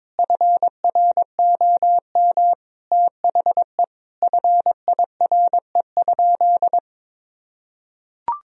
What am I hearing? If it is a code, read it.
Morse code, heard as a keyed tone, about 22 wpm, 700 Hz. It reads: FROM THE FIRE?
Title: Sentences from Top 1000 Words - Review Part 3 - 22wpm